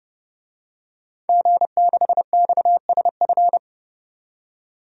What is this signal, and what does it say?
Morse code, heard as a keyed tone, about 32 wpm, 700 Hz. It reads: G6XSF
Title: DXCC-100 Call Signs - 5 Characters - 30wpm